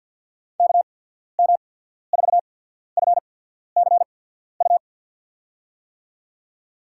Morse code, heard as a keyed tone, about 48 wpm, spaced as for 14 wpm, 690 Hz. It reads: KM4FCU
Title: USA Call Signs - 6 Characters - 8x Character Spacing - 50wpm